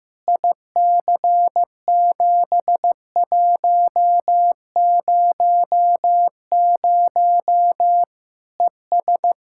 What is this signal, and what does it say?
Morse code, heard as a keyed tone, about 15 wpm, 700 Hz. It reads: IC7100 ES
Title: Practice QSO - Exchange 2 - Info - 15wpm